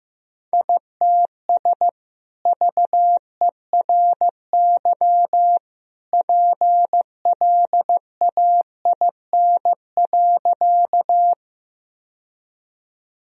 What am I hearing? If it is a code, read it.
Morse code, heard as a keyed tone, about 15 wpm, 700 Hz. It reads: ITS VERY PLAIN.
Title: Sentences from Top 1000 Words - Review Part 2 - 15wpm